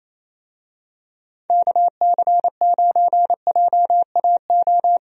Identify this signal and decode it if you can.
Morse code, heard as a keyed tone, about 28 wpm, 700 Hz. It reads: KC9JAO
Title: USA Call Signs - 6 Characters - 28wpm